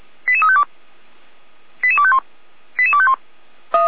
Five-tone selective call (ZVEI), heard as quick runs of five tones, tones of about 70 ms. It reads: 88352 88251 88251